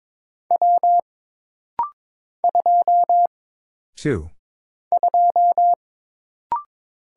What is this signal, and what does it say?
Morse code, heard as a keyed tone, about 22 wpm, 700 Hz. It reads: W 2 2
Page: This rapid-fire timing is geared to push your ICR (Instant Character Recognition) skills.